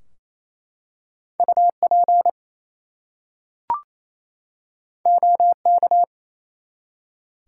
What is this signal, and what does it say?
Morse code, heard as a keyed tone, about 27 wpm, 700 Hz. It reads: UP OK
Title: Two Letter Words - 28wpm